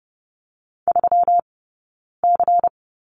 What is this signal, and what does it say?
Morse code, heard as a keyed tone, about 30 wpm, 700 Hz. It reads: 3 C
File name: Lesson 20d - Number 3 - Two Characters 30wpm